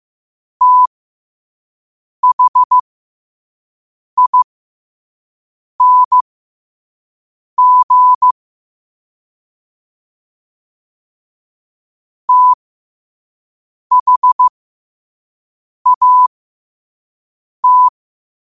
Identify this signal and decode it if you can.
Morse code, heard as a keyed tone, about 15 wpm, spaced as for 5 wpm, 1000 Hz. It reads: THING THAT
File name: qotdfiles / 5wpm